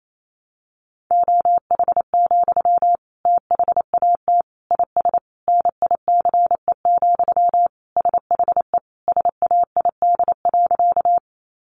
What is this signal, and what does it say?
Morse code, heard as a keyed tone, about 28 wpm, 700 Hz. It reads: OH, THAT IS NICE, SHE SAID.
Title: Sentences from Top 1100 Words 28wpm